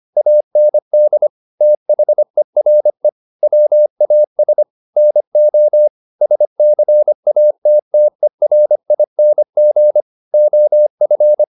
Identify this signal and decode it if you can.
Morse code, heard as a keyed tone, about 25 wpm, 600 Hz. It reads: AND THERE WAS NO SCATTERING OF
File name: the old man and the sea part 1 25wpm